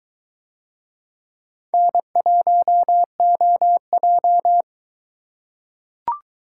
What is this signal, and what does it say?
Morse code, heard as a keyed tone, about 21 wpm, 700 Hz. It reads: N1OJ